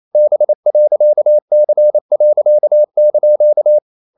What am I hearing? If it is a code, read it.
Morse code, heard as a keyed tone, about 28 wpm, 600 Hz. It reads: B.C.)
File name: CW / Practice / EnglishQuotations0128